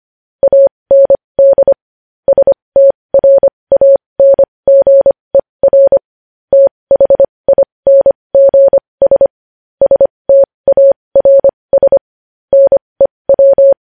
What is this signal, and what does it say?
Morse code, heard as a keyed tone, about 25 wpm, spaced as for 20 wpm, 550 Hz. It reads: AND STRANGER THINGS STARS NEW
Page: news headlines in morse code at 20 wpm